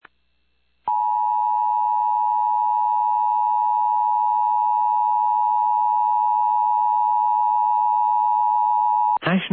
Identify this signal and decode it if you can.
Emergency Alert System (SAME) tones (the 853 and 960 Hz attention tone), with speech. It is heard from 0.8 to 9.2 s.
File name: eas / KUSO / KUSO